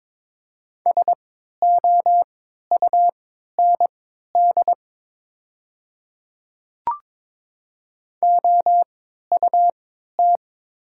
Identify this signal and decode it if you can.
Morse code, heard as a keyed tone, about 22 wpm, spaced as for 12 wpm, 700 Hz. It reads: SOUND OUT
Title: 3x Character Spacing - Top 100 Words - 22wpm